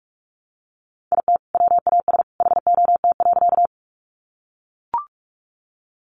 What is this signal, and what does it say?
Morse code, heard as a keyed tone, about 44 wpm, 700 Hz. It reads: IT WAS HOT.
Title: Sentences from Top 500 Words 45wpm